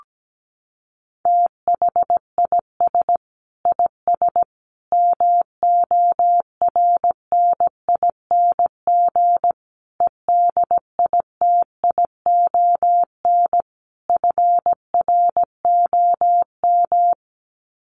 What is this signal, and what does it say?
Morse code, heard as a keyed tone, about 17 wpm, 700 Hz. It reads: THIS IS MORNING EDITION FROM
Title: Sets of 5 Words Encore2 17wpm